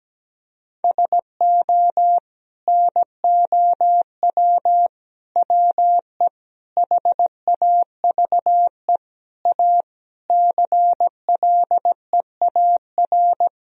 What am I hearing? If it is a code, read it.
Morse code, heard as a keyed tone, about 17 wpm, 700 Hz. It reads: SO NOW WE HAVE A CLEAR
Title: Sentences from Top 1000 Words - Review Part 2 - 17wpm